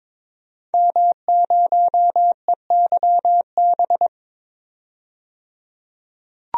Morse code, heard as a keyed tone, about 22 wpm, 700 Hz. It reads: M0EYB
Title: DXCC-100 Call Signs - 5 Characters - 22wpm